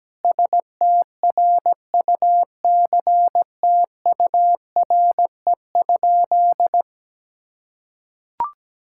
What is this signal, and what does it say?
Morse code, heard as a keyed tone, about 17 wpm, 700 Hz. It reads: STRUCTURE?
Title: Sentences from Top 1000 Words - Review Part 6 - 17wpm